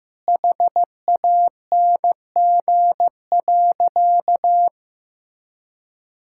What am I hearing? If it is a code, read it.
Morse code, heard as a keyed tone, about 15 wpm, 700 Hz. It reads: HANG.
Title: Sentences from Top 1600 Words 15wpm